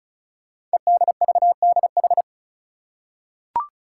Morse code, heard as a keyed tone, about 36 wpm, 700 Hz. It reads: EDVDH
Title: Code Groups - 5 Letters - 35wpm